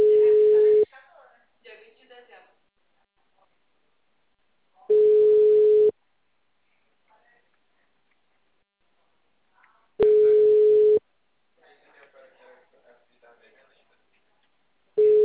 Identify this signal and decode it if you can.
Ringback tone (Europe, 425 Hz), with speech.
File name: out- -2055- - - .67003